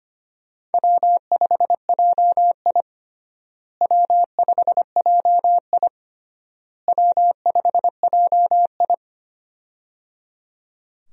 Morse code, heard as a keyed tone, about 25 wpm, 700 Hz. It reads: W5JI W5JI W5JI
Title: USA Call Signs - 4 Characters - Repeated 3x - 25wpm